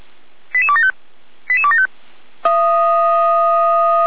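Five-tone selective call (ZVEI), heard as quick runs of five tones, tones of about 70 ms. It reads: 88275 88275
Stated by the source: ZVEI five-tone call received by SDR.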